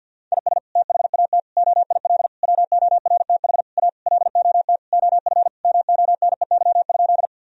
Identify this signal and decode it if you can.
Morse code, heard as a keyed tone, about 50 wpm, 700 Hz. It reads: IS THAT OIL WORTH A LOT OF MONEY?
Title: Sentences from Top 1000 Words - Review Part 4 - 50wpm